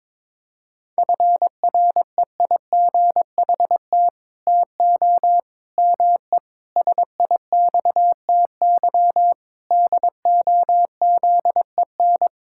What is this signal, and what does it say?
Morse code, heard as a keyed tone, about 22 wpm, 700 Hz. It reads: FREIGHT TO ME SIXTY DOZEN
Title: Pangram Challenge v2 - 22wpm